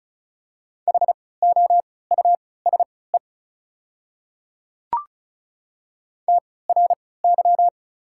Morse code, heard as a keyed tone, about 36 wpm, spaced as for 20 wpm, 690 Hz. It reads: HOUSE TRY
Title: 3x Character Spacing - Top 200 Words - 35wpm